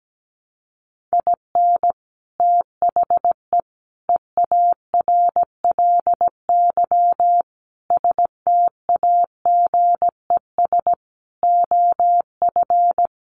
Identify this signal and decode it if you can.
Morse code, heard as a keyed tone, about 17 wpm, 700 Hz. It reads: IN THE EARLY STAGES OF
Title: Sets of 5 Words Encore 17wpm